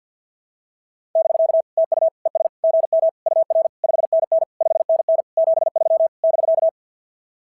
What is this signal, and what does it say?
Morse code, heard as a keyed tone, about 46 wpm, 660 Hz. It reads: <BK> TU ES GM UR 5NN 5NN 73 <BK>
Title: SOTA QSO Practice 50wpm